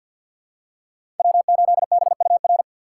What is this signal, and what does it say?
Morse code, heard as a keyed tone, about 50 wpm, 700 Hz. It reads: W8BUR
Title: USA Call Signs - 5 Characters - 2x Character Spacing - 50wpm